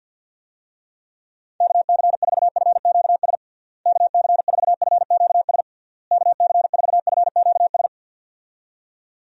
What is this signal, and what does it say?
Morse code, heard as a keyed tone, about 51 wpm, 700 Hz. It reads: KC4FQS KC4FQS KC4FQS
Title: USA Call Signs - 6 Characters - Repeated 3x - 50wpm